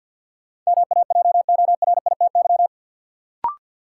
Morse code, heard as a keyed tone, about 52 wpm, 700 Hz. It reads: MAJORITY
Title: Sets of 3 Words - New Word Drill - 50wpm